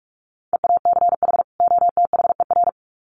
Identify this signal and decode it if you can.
Morse code, heard as a keyed tone, about 44 wpm, 700 Hz. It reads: EACH OTHER
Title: Sets of 2 Words 45wpm